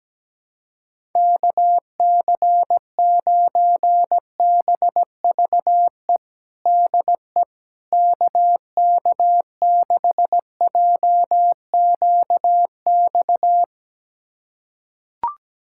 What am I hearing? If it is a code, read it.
Morse code, heard as a keyed tone, about 17 wpm, 700 Hz. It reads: KC9BVE DE KK6JQX
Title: USA Callsign DE Callsign - 6 Characters - 17wpm